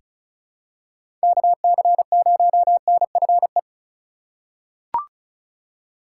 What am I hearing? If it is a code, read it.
Morse code, heard as a keyed tone, about 36 wpm, 700 Hz. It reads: KC0NFE
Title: USA Call Signs - 3x Character Spacing - 35wpm